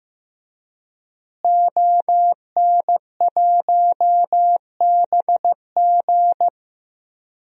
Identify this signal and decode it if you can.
Morse code, heard as a keyed tone, about 16 wpm, 700 Hz. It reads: ON1BG
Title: DXCC-100 Call Signs - 5 Characters - Repeated 3x - 15wpm